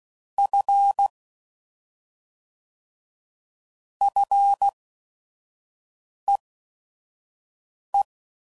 Morse code, heard as a keyed tone, about 16 wpm, spaced as for 5 wpm, 800 Hz. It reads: F FEE
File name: Prefixes / FLs & THs / Single Fs Ls